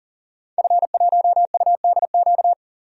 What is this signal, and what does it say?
Morse code, heard as a keyed tone, about 40 wpm, 700 Hz. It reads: F1UDQ
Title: DXCC-100 Call Signs - 5 Characters - 40wpm